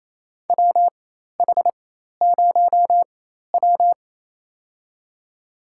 Morse code, heard as a keyed tone, about 28 wpm, spaced as for 13 wpm, 700 Hz. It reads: WH0W
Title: USA Call Signs - 4x Character Spacing - 28wpm